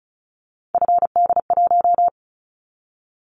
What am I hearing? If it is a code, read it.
Morse code, heard as a keyed tone, about 34 wpm, 700 Hz. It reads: FD1
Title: Callsign Prefixes for Top 25 DXCC - 35wpm